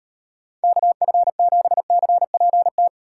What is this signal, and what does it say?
Morse code, heard as a keyed tone, about 38 wpm, 700 Hz. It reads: KF7CPT